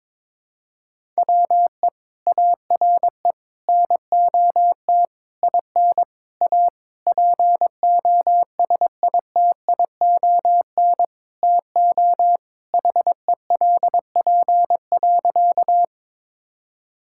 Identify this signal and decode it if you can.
Morse code, heard as a keyed tone, about 22 wpm, 700 Hz. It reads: WE ARE NOT IN A POSITION TO HELP.
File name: Sentences with 7-Word Phrases - 22wpm